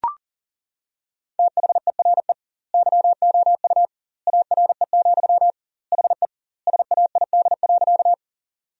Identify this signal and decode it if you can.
Morse code, heard as a keyed tone, about 40 wpm, 700 Hz. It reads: THERE YOU ARE, HE SAID.
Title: 2x Word-Spacing - Sentences from Top 100 Words - 40wpm